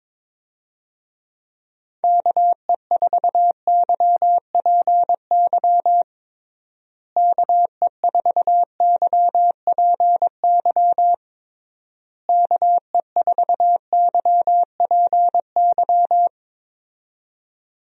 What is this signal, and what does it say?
Morse code, heard as a keyed tone, about 22 wpm, 700 Hz. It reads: KE4YPY KE4YPY KE4YPY
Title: USA Call Signs - 6 Characters - Repeated 3x - 22wpm